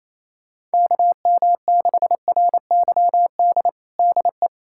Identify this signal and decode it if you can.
Morse code, heard as a keyed tone, about 28 wpm, 700 Hz. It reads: KM6RYD DE
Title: USA Callsign DE Callsign - 6 Characters - 28wpm